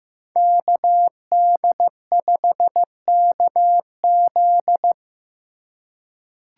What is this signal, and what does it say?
Morse code, heard as a keyed tone, about 15 wpm, 700 Hz. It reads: KD5KZ
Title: USA Call Signs - 5 Characters - 2x Character Spacing - 15wpm